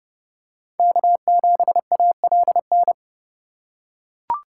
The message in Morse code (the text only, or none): K7ALN